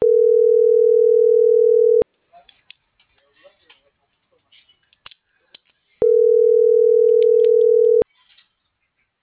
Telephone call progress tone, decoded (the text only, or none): ringback tone